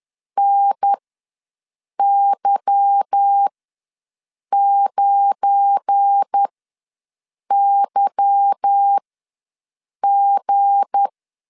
Morse code, heard as a keyed tone, about 10 wpm, spaced as for 6 wpm, 800 Hz. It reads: NY9YG